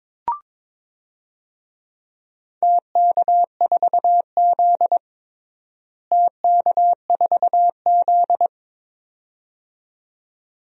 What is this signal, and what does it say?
Morse code, heard as a keyed tone, about 22 wpm, 700 Hz. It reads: TK4Z TK4Z
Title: DXCC-100 Call Signs - 4 Characters - Repeated 2x - 22wpm